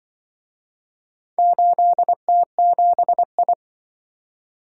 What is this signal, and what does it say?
Morse code, heard as a keyed tone, about 25 wpm, 700 Hz. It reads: 8T7I
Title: Speed-Racing - Four Letter-Number Combinations - 20wpm